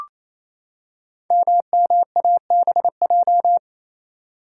Morse code, heard as a keyed tone, about 28 wpm, 700 Hz. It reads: MMABJ
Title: Code Groups - 5 Letter:Numbers - 28wpm